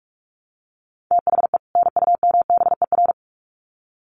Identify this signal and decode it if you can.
Morse code, heard as a keyed tone, about 44 wpm, 700 Hz. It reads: THE NUMBER
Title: Sets of 2 Words 45wpm